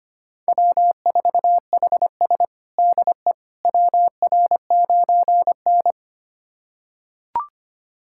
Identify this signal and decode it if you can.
Morse code, heard as a keyed tone, about 25 wpm, 700 Hz. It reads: W4HS DE WR9N
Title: USA Callsign DE Callsign - 4 Characters - 25wpm